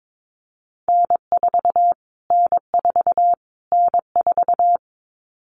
Morse code, heard as a keyed tone, about 22 wpm, 700 Hz. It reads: N4 N4 N4
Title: Callsign Prefixes for Top 25 DXCC - Repeated 3x - 22wpm